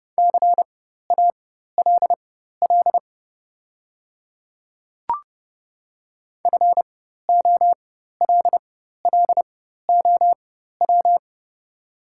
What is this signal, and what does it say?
Morse code, heard as a keyed tone, about 32 wpm, spaced as for 14 wpm, 700 Hz. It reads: CALL FOLLOW
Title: 4x Character Spacing - Top 300 Words - 30wpm